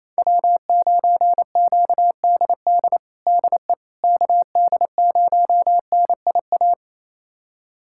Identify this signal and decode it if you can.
Morse code, heard as a keyed tone, about 28 wpm, 700 Hz. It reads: W9QDD DE KD0NIA
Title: USA Callsign DE Callsign 28wpm